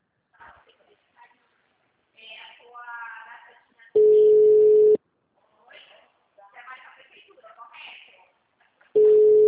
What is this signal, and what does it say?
Ringback tone (Europe, 425 Hz), with speech.